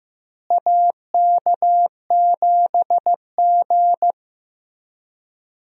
Morse code, heard as a keyed tone, about 15 wpm, 700 Hz. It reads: AK7G
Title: USA Call Signs - 4 Characters - Repeated 2x - 15wpm